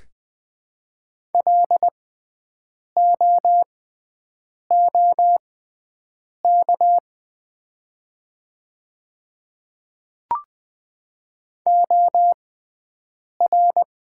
Morse code, heard as a keyed tone, about 20 wpm, spaced as for 7 wpm, 700 Hz. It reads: LOOK OR
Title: 6x Character Spacing - Top 100 Words - 20wpm